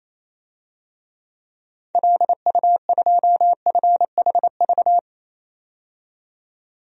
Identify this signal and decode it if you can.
Morse code, heard as a keyed tone, about 28 wpm, 700 Hz. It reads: LU2FHV
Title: DXCC-100 Call Signs - 6 Characters - 28wpm